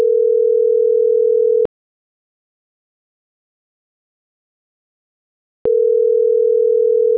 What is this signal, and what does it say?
Ringback tone (North America), 440 + 480 Hz.